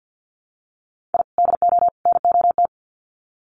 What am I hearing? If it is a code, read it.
Morse code, heard as a keyed tone, about 50 wpm, 700 Hz. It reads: I DO NOT
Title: Sets of 3 Words 50wpm